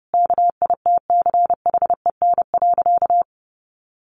Morse code, heard as a keyed tone, about 28 wpm, 700 Hz. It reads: KITCHEN.
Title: Sentences from Top 1400 Words 30wpm